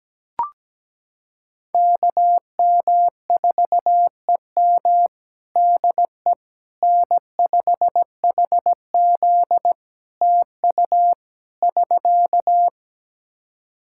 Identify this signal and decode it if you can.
Morse code, heard as a keyed tone, about 17 wpm, 700 Hz. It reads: KM4EM DE N5HZ TU <SK>
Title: Practice QSO - Exchange 4 - Ending - 17wpm